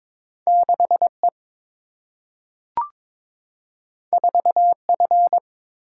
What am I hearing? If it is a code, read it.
Morse code, heard as a keyed tone, about 22 wpm, 700 Hz. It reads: 6E 4F
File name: practice / 22wpm / Two Letter-Number Combinations - 22wpm